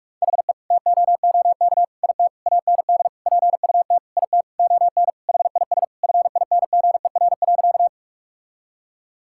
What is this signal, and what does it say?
Morse code, heard as a keyed tone, about 46 wpm, 700 Hz. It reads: HE TOOK IT AND PUT IT ON HIS FINGER.
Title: Sentences from Top 1600 Words 45wpm